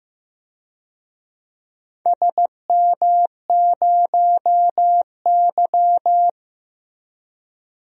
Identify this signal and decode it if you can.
Morse code, heard as a keyed tone, about 15 wpm, 700 Hz. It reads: SM0Y
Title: DXCC-100 Call Signs - 4 Characters - Repeated 2x - 15wpm